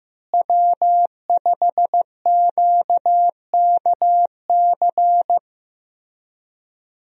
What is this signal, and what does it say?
Morse code, heard as a keyed tone, about 16 wpm, 700 Hz. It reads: W5QKC
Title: USA Call Signs - 5 Characters - Repeated 3x - 15wpm